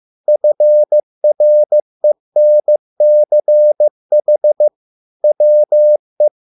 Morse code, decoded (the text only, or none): FRENCH WE